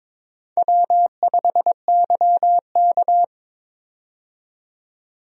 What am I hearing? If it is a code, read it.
Morse code, heard as a keyed tone, about 22 wpm, 700 Hz. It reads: W5YK
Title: USA Call Signs - 4 Characters - 22wpm